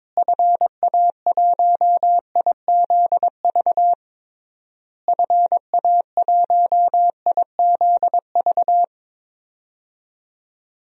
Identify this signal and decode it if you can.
Morse code, heard as a keyed tone, about 22 wpm, 700 Hz. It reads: FA1IZV FA1IZV
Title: DXCC-100 Call Signs - 6 Characters - Repeated 3x - 22wpm